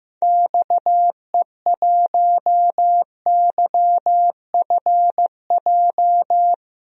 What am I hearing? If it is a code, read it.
Morse code, heard as a keyed tone, about 15 wpm, 700 Hz. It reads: XE1YFJ
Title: DXCC-100 Call Signs - 6 Characters - 15wpm